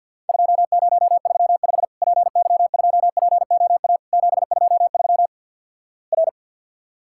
Morse code, heard as a keyed tone, about 50 wpm, 700 Hz. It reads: 2035 PY2POA 723 R
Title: Worked All Europe DX Contest - QTC Exchange - 50wpm